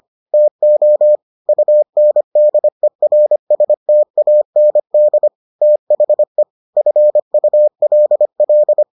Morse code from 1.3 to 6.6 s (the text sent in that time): UNDERSTAND THE